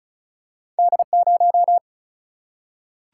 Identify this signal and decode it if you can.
Morse code, heard as a keyed tone, about 36 wpm, 700 Hz. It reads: D0
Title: Single-Letter Single-Number 35wpm